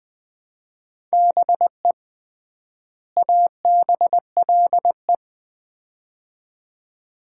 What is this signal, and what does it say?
Morse code, heard as a keyed tone, about 20 wpm, 700 Hz. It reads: BE ABLE
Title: 3x Word-Spacing - Sets of 2 Words - 20wpm